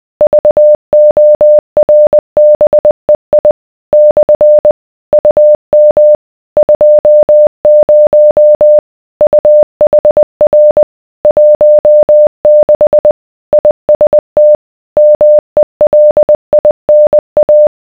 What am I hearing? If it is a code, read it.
Morse code, heard as a keyed tone, about 20 wpm, 600 Hz. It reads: VORBEI / UM 20 UHR 16 IST MELINA